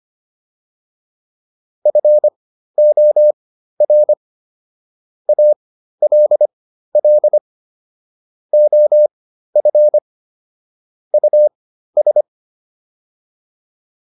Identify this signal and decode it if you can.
Morse code, heard as a keyed tone, about 25 wpm, spaced as for 13 wpm, 600 Hz. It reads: FOR ALL OF US